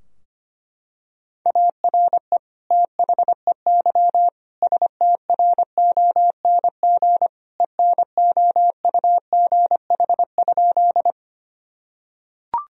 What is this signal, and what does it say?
Morse code, heard as a keyed tone, about 25 wpm, 700 Hz. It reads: ARE THEY STRONG ENOUGH?
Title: Sentences from Top 1000 Words - Review Part 3 - 25wpm